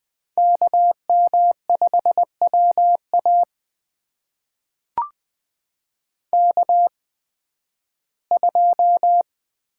Morse code, heard as a keyed tone, about 20 wpm, spaced as for 7 wpm, 700 Hz. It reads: KM5W AK2
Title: USA Call Signs - 8x Character Spacing - 20wpm